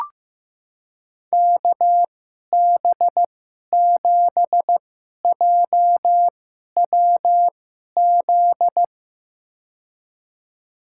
Morse code, heard as a keyed tone, about 15 wpm, spaced as for 11 wpm, 700 Hz. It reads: KB7JWZ
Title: USA Call Signs - 2x Character Spacing - 15wpm